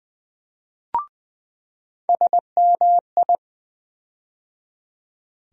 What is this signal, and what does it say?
Morse code, heard as a keyed tone, about 20 wpm, 700 Hz. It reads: SMI
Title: USA Call Sign Postfixes - 3 Letters 20wpm